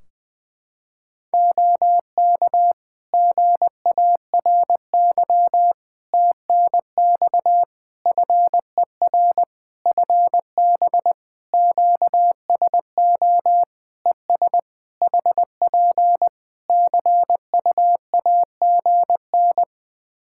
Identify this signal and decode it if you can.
Morse code, heard as a keyed tone, about 20 wpm, 700 Hz. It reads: OK GARY TNX FER FB QSO ES HP CUAGN
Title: Practice QSO - Exchange 4 - Ending - 20wpm